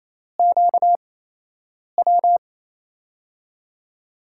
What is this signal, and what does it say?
Morse code, heard as a keyed tone, about 28 wpm, spaced as for 8 wpm, 700 Hz. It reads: QW
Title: USA Call Signs - 6 Characters - 8x Character Spacing - 28wpm